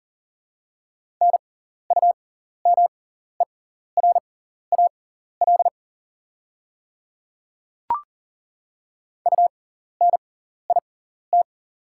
Morse code, heard as a keyed tone, about 41 wpm, spaced as for 14 wpm, 690 Hz. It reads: NUMERAL UNIT
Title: 6x Character Spacing - Top 500 Words - 40wpm